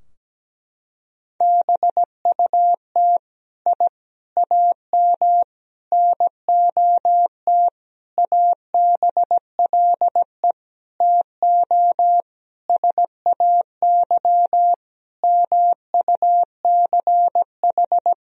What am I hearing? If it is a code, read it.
Morse code, heard as a keyed tone, about 17 wpm, 700 Hz. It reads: BUT I AM NOT ABLE TO SAY MUCH
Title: Sentences from Top 1000 Words - Review Part 2 - 17wpm